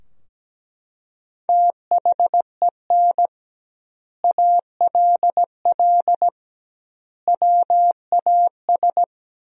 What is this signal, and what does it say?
Morse code, heard as a keyed tone, about 17 wpm, 700 Hz. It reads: THEN ALL WAS
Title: 2x Word-Spacing - Sentences from Top 500 Words - 17wpm